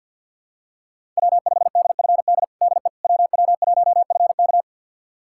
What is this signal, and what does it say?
Morse code, heard as a keyed tone, about 49 wpm, 700 Hz. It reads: W5DFD DE WW1FK
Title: USA Callsign DE Callsign - 5 Characters - 50wpm